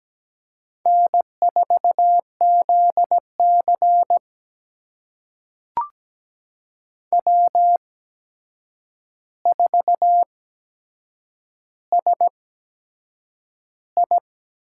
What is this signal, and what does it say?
Morse code, heard as a keyed tone, about 17 wpm, spaced as for 7 wpm, 700 Hz. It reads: N4ZC W4SI
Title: USA Call Signs - 4 Characters - 8x Character Spacing - 17wpm